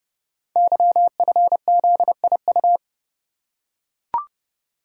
Morse code, heard as a keyed tone, about 30 wpm, 700 Hz. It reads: YFZIU
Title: Code Groups - 5 Letters - 30wpm